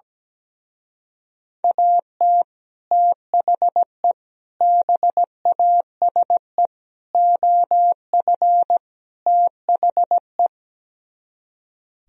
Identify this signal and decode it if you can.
Morse code, heard as a keyed tone, about 17 wpm, 700 Hz. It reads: AT THE BASE OF THE
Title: Speed-Racing - Sets of 5 Words 17wpm